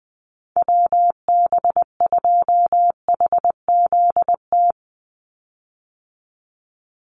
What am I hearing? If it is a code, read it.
Morse code, heard as a keyed tone, about 20 wpm, 700 Hz. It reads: WB2HZT
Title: USA Call Signs 20wpm